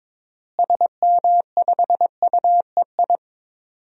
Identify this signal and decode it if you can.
Morse code, heard as a keyed tone, about 22 wpm, 700 Hz. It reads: SM5UEI
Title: DXCC-100 Call Signs - 6 Characters - Repeated 2x - 22wpm